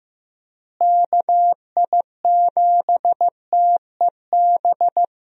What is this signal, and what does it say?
Morse code, heard as a keyed tone, about 15 wpm, 700 Hz. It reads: KI7TEB
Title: USA Call Signs - 6 Characters - Repeated 3x - 15wpm